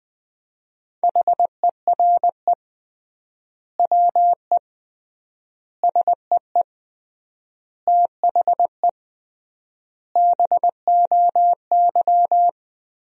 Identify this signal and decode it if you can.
Morse code, heard as a keyed tone, about 20 wpm, 700 Hz. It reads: HERE WE SEE THE BOY
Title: 3x Word-Spacing - Sentences from Top 300 Words - 20wpm